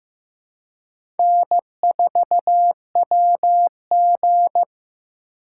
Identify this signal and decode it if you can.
Morse code, heard as a keyed tone, about 15 wpm, 700 Hz. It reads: N4WG
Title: USA Call Signs - 2x Character Spacing - 15wpm